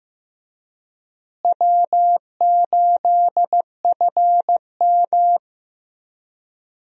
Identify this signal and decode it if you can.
Morse code, heard as a keyed tone, about 15 wpm, 700 Hz. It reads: W8FM
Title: USA Call Signs - 4 Characters - Repeated 2x - 15wpm